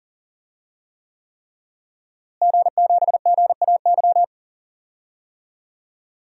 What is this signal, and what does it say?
Morse code, heard as a keyed tone, about 40 wpm, 700 Hz. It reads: G7GAY
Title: DXCC-100 Call Signs - 5 Characters - Repeated 3x - 40wpm